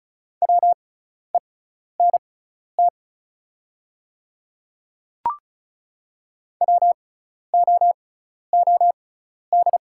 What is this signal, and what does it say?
Morse code, heard as a keyed tone, about 36 wpm, spaced as for 12 wpm, 690 Hz. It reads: WENT WOOD
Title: 6x Character Spacing - Top 300 Words - 35wpm